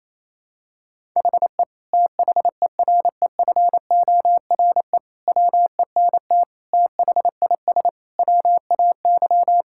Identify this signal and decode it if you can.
Morse code, heard as a keyed tone, about 28 wpm, 700 Hz. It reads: HE THEREFORE WENT THIS WAY
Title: Sentences from Top 1000 Words - Review Part 5 - 28wpm